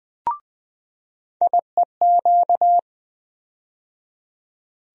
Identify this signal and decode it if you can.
Morse code, heard as a keyed tone, about 20 wpm, 700 Hz. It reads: IEQ
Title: USA Call Sign Postfixes - 3 Letters 20wpm